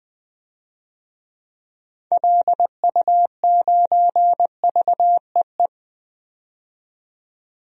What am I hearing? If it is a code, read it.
Morse code, heard as a keyed tone, about 20 wpm, 700 Hz. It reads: LU9VEE
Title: DXCC-100 Call Signs - 6 Characters - 20wpm